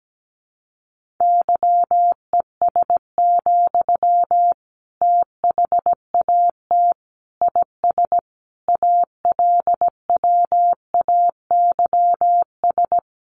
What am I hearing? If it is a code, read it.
Morse code, heard as a keyed tone, about 17 wpm, 700 Hz. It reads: YES, THAT IS ALWAYS
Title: Sentences from Top 1500 Words 17wpm